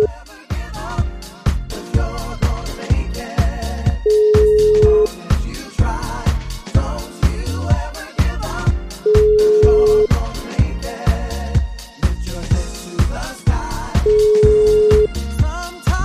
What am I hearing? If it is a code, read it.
Ringback tone (Europe, 425 Hz).